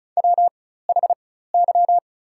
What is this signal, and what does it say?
Morse code, heard as a keyed tone, about 36 wpm, spaced as for 16 wpm, 700 Hz. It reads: WHY
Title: 4x Character Spacing - Top 200 Words - 35wpm